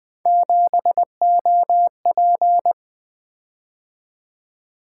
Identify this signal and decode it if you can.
Morse code, heard as a keyed tone, about 20 wpm, 700 Hz. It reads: 7OP
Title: Three Letter-Number Combinations - 20wpm